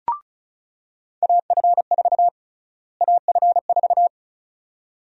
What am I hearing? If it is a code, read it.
Morse code, heard as a keyed tone, about 36 wpm, 700 Hz. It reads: AF4 AF4
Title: USA Call Sign Prefixes - Repeated 2x - 35wpm